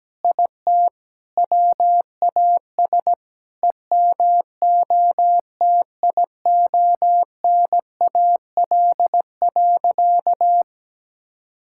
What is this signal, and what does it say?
Morse code, heard as a keyed tone, about 17 wpm, 700 Hz. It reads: IT WAS EMOTIONAL.
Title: Sentences from Top 1000 Words - Review Part 5 - 17wpm